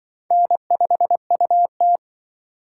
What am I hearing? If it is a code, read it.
Morse code, heard as a keyed tone, about 25 wpm, 700 Hz. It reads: N5UT
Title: Speed Racing - USA 1x2 Call Signs 20wpm